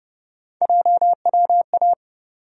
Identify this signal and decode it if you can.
Morse code, heard as a keyed tone, about 30 wpm, 700 Hz. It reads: JWA
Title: USA Call Sign Postfixes - 3 Letters 30wpm